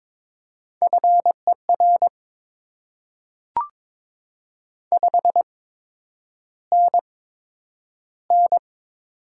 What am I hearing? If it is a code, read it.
Morse code, heard as a keyed tone, about 22 wpm, spaced as for 9 wpm, 700 Hz. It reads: FER 5NN
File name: Top 20 QSO Elements - 8x Character Spacing - 22wpm